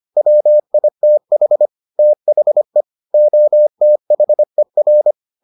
Morse code, heard as a keyed tone, about 25 wpm, 600 Hz. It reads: WITH THE OTHER